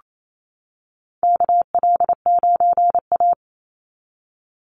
Morse code, heard as a keyed tone, about 27 wpm, 700 Hz. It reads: KL9A
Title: Call Signs - 28wpm